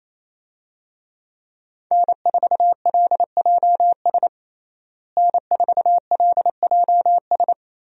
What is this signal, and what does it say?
Morse code, heard as a keyed tone, about 28 wpm, 700 Hz. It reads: N4LJS N4LJS
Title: USA Call Signs - 5 Characters - Repeated 3x - 28wpm